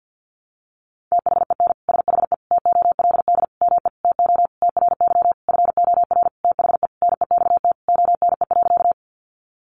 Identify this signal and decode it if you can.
Morse code, heard as a keyed tone, about 50 wpm, 700 Hz. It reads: THEN SHE TOLD ME TO TRY FOR THE NEXT ONE.
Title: Sentences from Top 400 Words 50wpm